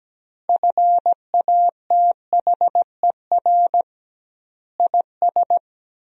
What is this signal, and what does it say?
Morse code, heard as a keyed tone, about 17 wpm, 700 Hz. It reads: FATHER IS
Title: 2x Word-Spacing - Sentences from Top 300 Words - 17wpm